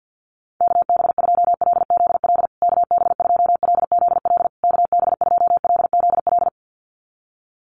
Morse code, heard as a keyed tone, about 50 wpm, 700 Hz. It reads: KB2LZL KB2LZL KB2LZL
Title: USA Call Signs Repeated 3x 50wpm